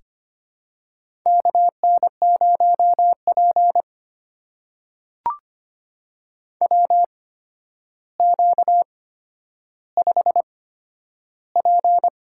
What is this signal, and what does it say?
Morse code, heard as a keyed tone, about 25 wpm, spaced as for 11 wpm, 700 Hz. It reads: KN0P WQ5P
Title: USA Call Signs - 4 Characters - 8x Character Spacing - 25wpm